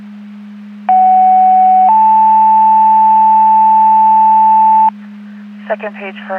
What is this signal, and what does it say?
Two-tone page, heard as a one-second tone, then a three-second tone, about 750 then 880 Hz, with speech. It is heard from 0.9 to 4.9 s.